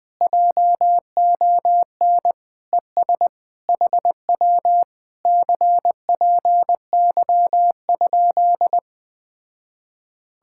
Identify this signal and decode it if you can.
Morse code, heard as a keyed tone, about 20 wpm, 700 Hz. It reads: JON ES HW CPY?
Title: Speed-Racing Common QSO Phrases - 17wpm